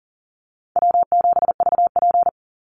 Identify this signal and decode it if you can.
Morse code, heard as a keyed tone, about 40 wpm, 700 Hz. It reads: W7VP
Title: Easy Call Signs Repeated 3x - 40wpm